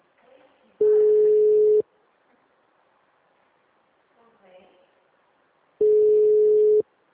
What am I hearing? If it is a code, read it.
Ringback tone (Europe, 425 Hz), with speech.